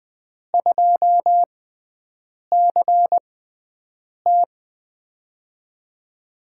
Morse code, heard as a keyed tone, about 20 wpm, spaced as for 7 wpm, 700 Hz. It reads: 2CT
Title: USA Call Signs - 5 Characters - 6x Character Spacing - 20wpm